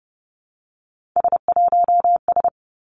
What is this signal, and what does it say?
Morse code, heard as a keyed tone, about 30 wpm, 700 Hz. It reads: S1S
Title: Three Challenging Letter-Number Combinations 30wpm